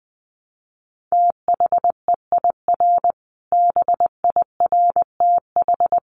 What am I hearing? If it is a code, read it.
Morse code, heard as a keyed tone, about 20 wpm, 700 Hz. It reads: THEIR BIRTH
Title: Sentences from Top 900 Words 20wpm